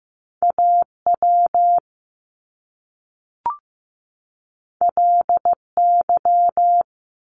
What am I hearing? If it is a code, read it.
Morse code, heard as a keyed tone, about 15 wpm, 700 Hz. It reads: AW LY
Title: Top Two Letter Combinations in Words 15wpm